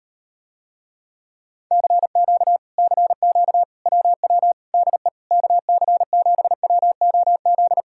Morse code, heard as a keyed tone, about 38 wpm, 700 Hz. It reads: CQ CQ WW DE KC7WOZ